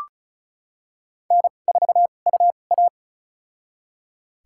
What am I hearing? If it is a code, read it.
Morse code, heard as a keyed tone, about 34 wpm, spaced as for 25 wpm, 700 Hz. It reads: N4UA
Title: USA Call Signs - 4 Characters - 2x Character Spacing - 35wpm